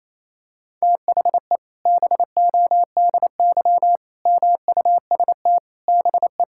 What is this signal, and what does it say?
Morse code, heard as a keyed tone, about 28 wpm, 700 Hz. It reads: THE BODY MUST BE
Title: Sentences from Top 1000 Words - Review Part 5 - 28wpm